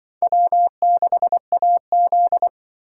Morse code, heard as a keyed tone, about 23 wpm, 700 Hz. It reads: W6AZ
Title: Speed Racing - USA 1x2 Call Signs 20wpm